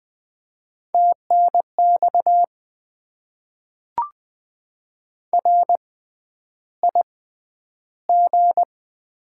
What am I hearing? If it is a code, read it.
Morse code, heard as a keyed tone, about 20 wpm, spaced as for 10 wpm, 700 Hz. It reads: TNX RIG
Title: Top 20 QSO Elements - 6x Character Spacing - 20wpm